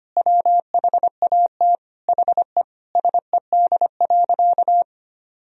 Morse code, heard as a keyed tone, about 25 wpm, 700 Hz. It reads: WHAT HE SED.
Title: Sentences from Top 400 Words and Abbreviations 25wpm